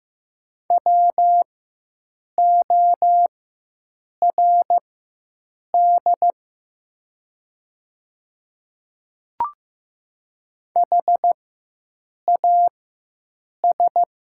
Morse code, heard as a keyed tone, about 15 wpm, spaced as for 7 wpm, 700 Hz. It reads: WORD HAS
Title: 4x Character Spacing - Top 100 Words - 15wpm